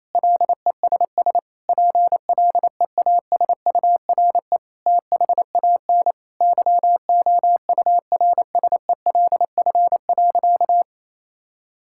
Morse code, heard as a keyed tone, about 28 wpm, 700 Hz. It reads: LESS PLEASURE THAN YOURSELF.
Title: Sentences from Top 1600 Words 28wpm